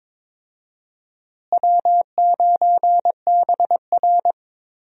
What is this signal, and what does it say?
Morse code, heard as a keyed tone, about 21 wpm, 700 Hz. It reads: W9BR
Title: USA Call Signs - 4 Characters - Repeated 3x - 22wpm